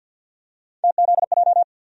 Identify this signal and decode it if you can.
Morse code, heard as a keyed tone, about 49 wpm, 700 Hz. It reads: TZJ
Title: USA Call Sign Postfixes - 3 Letters 50wpm